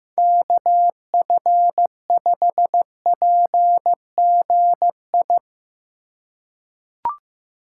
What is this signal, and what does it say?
Morse code, heard as a keyed tone, about 15 wpm, 700 Hz. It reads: KF5PGI
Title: USA Call Signs - 6 Characters - 3x Character Spacing - 15wpm